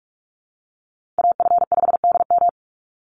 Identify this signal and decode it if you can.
Morse code, heard as a keyed tone, about 44 wpm, 700 Hz. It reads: AF5DM
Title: Call Signs Repeated 2x - 45wpm